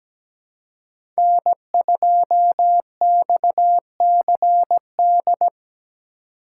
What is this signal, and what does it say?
Morse code, heard as a keyed tone, about 17 wpm, 700 Hz. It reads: N2XCD
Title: USA Call Signs - 5 Characters - 3x Character Spacing - 17wpm